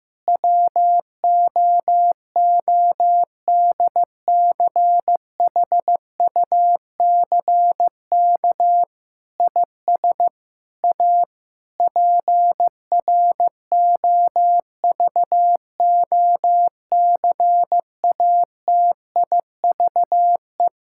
Morse code, heard as a keyed tone, about 15 wpm, 700 Hz. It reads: WOODCHUCK IS A PROVOCATIVE
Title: Pangram Challenge v2 - 15wpm